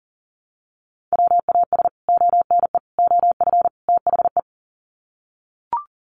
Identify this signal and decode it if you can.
Morse code, heard as a keyed tone, about 40 wpm, 700 Hz. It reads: WAS ONE OF THE
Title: Sets of 4 Words 40wpm